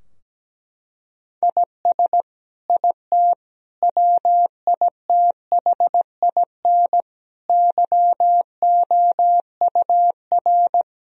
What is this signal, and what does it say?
Morse code, heard as a keyed tone, about 17 wpm, 700 Hz. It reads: IS IT WITHIN YOUR
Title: Sentences from Top 1000 Words - Review Part 4 - 17wpm